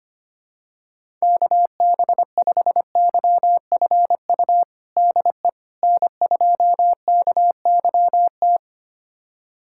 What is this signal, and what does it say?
Morse code, heard as a keyed tone, about 25 wpm, 700 Hz. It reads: KB5YFU DE N2KYT
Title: USA Callsign DE Callsign 25wpm